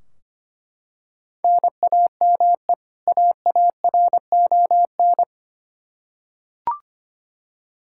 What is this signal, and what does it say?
Morse code, heard as a keyed tone, about 25 wpm, 700 Hz. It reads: NAME AARON
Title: Practice QSO - Exchange 1 - Beginning - 25wpm